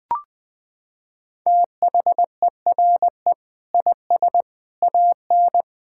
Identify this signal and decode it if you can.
Morse code, heard as a keyed tone, about 20 wpm, 700 Hz. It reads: THERE IS AN
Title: Sentences with 8-Word Phrases - 20wpm